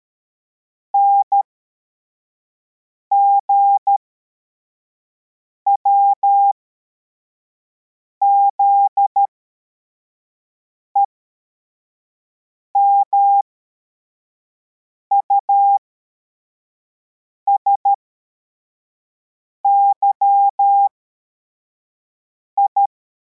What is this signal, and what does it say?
Morse code, heard as a keyed tone, about 13 wpm, spaced as for 4 wpm, 800 Hz. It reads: NGWZE MUSYI